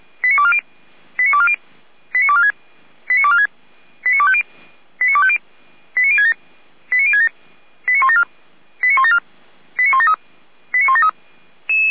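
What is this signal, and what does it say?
Five-tone selective call (ZVEI), heard as quick runs of five tones, tones of about 70 ms. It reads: 89240 89240 89246 89246 89250 89250 89067 89067 89174 89174 89173 89173
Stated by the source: ZVEI five-tone selective call received with an SDR, from feuerwehr-orlamuende.org, the Orlamünde fire brigade.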